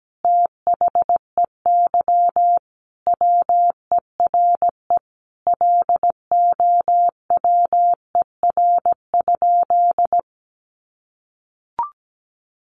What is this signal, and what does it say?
Morse code, heard as a keyed tone, about 17 wpm, 700 Hz. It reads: THEY WERE LOWER?
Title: Sentences from Top 700 Words 17wpm